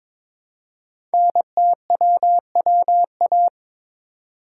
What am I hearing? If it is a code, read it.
Morse code, heard as a keyed tone, about 22 wpm, 700 Hz. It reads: NTWWA
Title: Code Groups - 5 Letters - 22wpm